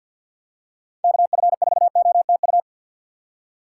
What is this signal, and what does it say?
Morse code, heard as a keyed tone, about 53 wpm, 700 Hz. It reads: KF4OTU
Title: USA Call Signs - 6 Characters - 6x Character Spacing - 50wpm